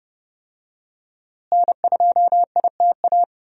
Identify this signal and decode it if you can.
Morse code, heard as a keyed tone, about 30 wpm, 700 Hz. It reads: N2ITA
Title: USA Call Signs - 5 Characters - Repeated 3x - 30wpm